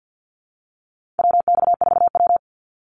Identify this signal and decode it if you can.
Morse code, heard as a keyed tone, about 50 wpm, 700 Hz. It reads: WX4W
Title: Call Signs - 50wpm